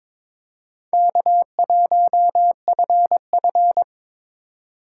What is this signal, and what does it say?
Morse code, heard as a keyed tone, about 22 wpm, 700 Hz. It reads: K1FF
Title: USA Call Signs - 4 Characters - 4x Character Spacing - 22wpm